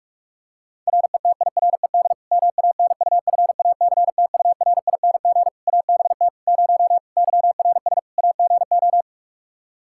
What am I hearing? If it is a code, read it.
Morse code, heard as a keyed tone, about 42 wpm, 700 Hz. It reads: RETIRED MANUFACTURING ABT 0 YRS AGO
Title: Practice QSO - Exchange 3 - More Info - 45wpm